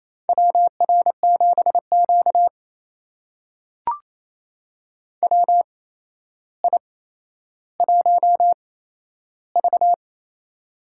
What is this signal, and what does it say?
Morse code, heard as a keyed tone, about 28 wpm, spaced as for 12 wpm, 700 Hz. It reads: WR7Q WI1V
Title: USA Call Signs - 4 Characters - 8x Character Spacing - 28wpm